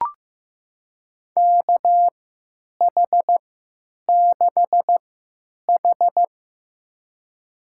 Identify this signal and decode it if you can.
Morse code, heard as a keyed tone, about 15 wpm, spaced as for 9 wpm, 700 Hz. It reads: KH6H
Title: USA Call Signs - 3x Character Spacing - 15wpm